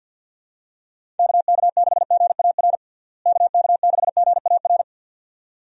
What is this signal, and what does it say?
Morse code, heard as a keyed tone, about 48 wpm, 700 Hz. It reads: KK6GAR KK6GAR
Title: USA Call Signs - 6 Characters - Repeated 2x - 50wpm